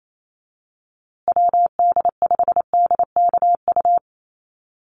Morse code, heard as a keyed tone, about 28 wpm, 700 Hz. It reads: WD5DKU
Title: USA Call Signs Repeated 3x 28wpm